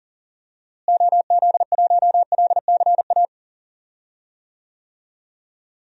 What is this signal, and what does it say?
Morse code, heard as a keyed tone, about 40 wpm, 700 Hz. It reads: OZ1LCA
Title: DXCC-100 Call Signs - 6 Characters - Repeated 2x - 40wpm